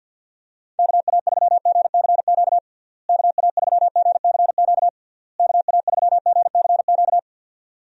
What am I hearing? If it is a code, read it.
Morse code, heard as a keyed tone, about 51 wpm, 700 Hz. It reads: KA3GCQ KA3GCQ KA3GCQ
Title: USA Call Signs - 6 Characters - Repeated 3x - 50wpm